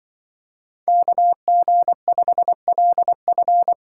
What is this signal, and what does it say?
Morse code, heard as a keyed tone, about 23 wpm, 700 Hz. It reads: KG5LF
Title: Speed Racing - USA 2x2 Call Signs 20wpm